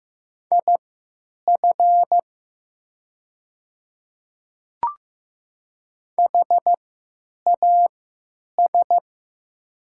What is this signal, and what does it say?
Morse code, heard as a keyed tone, about 15 wpm, spaced as for 9 wpm, 700 Hz. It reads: IF HAS
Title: 3x Character Spacing - Top 100 Words - 15wpm